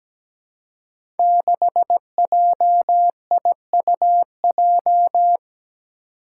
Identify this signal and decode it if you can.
Morse code, heard as a keyed tone, about 17 wpm, 700 Hz. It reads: 6JIUJ
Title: Five Challenging Letter-Number Combinations 17wpm